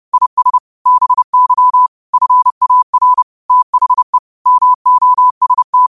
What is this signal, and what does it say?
Morse code, heard as a keyed tone, about 30 wpm, 1000 Hz. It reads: IS BY FAR THE MOST
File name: qotdfiles / 30wpm